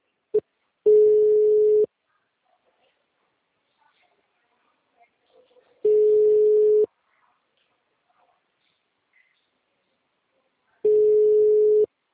Ringback tone (Europe, 425 Hz).